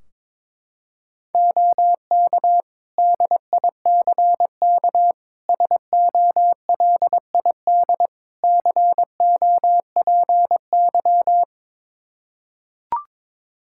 Morse code, heard as a keyed tone, about 22 wpm, 700 Hz. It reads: OK DICK SOLID COPY
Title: Practice QSO - Exchange 3 - More Info - 22wpm